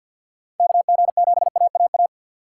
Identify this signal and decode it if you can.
Morse code, heard as a keyed tone, about 51 wpm, 700 Hz. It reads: KG7AAA
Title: USA Call Signs - 6 Characters - 4x Character Spacing - 50wpm